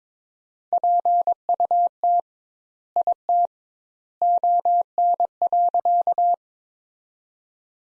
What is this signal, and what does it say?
Morse code, heard as a keyed tone, about 22 wpm, 700 Hz. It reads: PUT IT ON.
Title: Sentences Top 200 Words - 22wpm